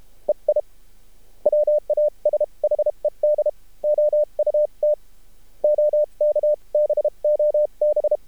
Morse code, heard as a keyed tone, about 32 wpm, spaced as for 27 wpm, 600 Hz. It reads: EI WASHED OUT OK BOB
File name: archived / CRSnet